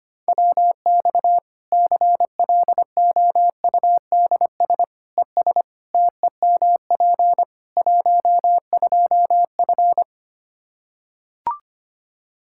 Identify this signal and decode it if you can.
Morse code, heard as a keyed tone, about 25 wpm, 700 Hz. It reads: WX CLOUDS ES TEMP 12F
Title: K1USN SST QSO Practice 25wpm